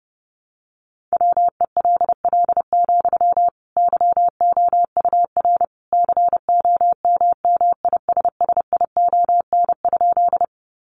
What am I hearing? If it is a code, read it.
Morse code, heard as a keyed tone, about 30 wpm, 700 Hz. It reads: WELL, YOUR COMMISSION?
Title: Sentences from Top 1400 Words 30wpm